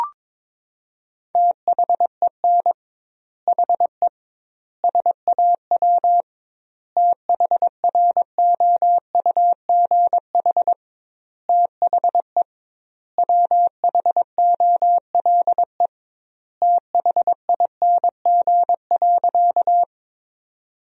Morse code, heard as a keyed tone, about 22 wpm, 700 Hz. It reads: THEN HE SAW THROUGH THE WHOLE THING.
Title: 2x Word-Spacing - Sentences from Top 400 Words - 22wpm